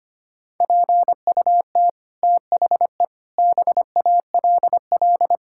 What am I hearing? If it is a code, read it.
Morse code, heard as a keyed tone, about 25 wpm, 700 Hz. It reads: PUT THE BALL
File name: Sets of 9 Words - 25wpm